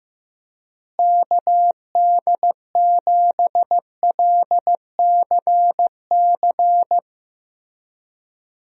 Morse code, heard as a keyed tone, about 16 wpm, 700 Hz. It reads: KD7LCC
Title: USA Call Signs - 6 Characters - Repeated 2x - 15wpm